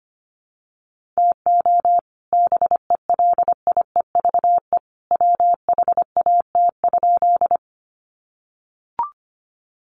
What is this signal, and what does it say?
Morse code, heard as a keyed tone, about 25 wpm, 700 Hz. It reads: TO BELIEVE WHAT?
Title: Sentences from Top 1000 Words 25wpm